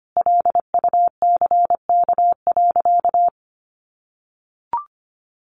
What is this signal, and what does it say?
Morse code, heard as a keyed tone, about 25 wpm, 700 Hz. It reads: LUCK.
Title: Sentences from Top 1500 Words 25wpm